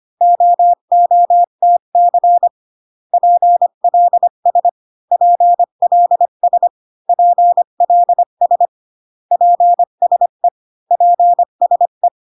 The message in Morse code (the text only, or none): OOTC PLS PLS PLS PSE PSE